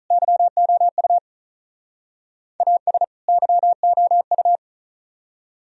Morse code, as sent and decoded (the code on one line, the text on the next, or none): -.-- --- ..- / .- ... / -.-- --- ..-
YOU AS YOU